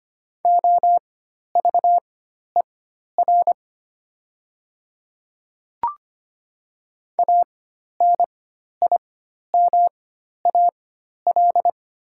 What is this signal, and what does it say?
Morse code, heard as a keyed tone, about 25 wpm, spaced as for 12 wpm, 700 Hz. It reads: OVER ANIMAL
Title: 4x Character Spacing - Top 200 Words - 25wpm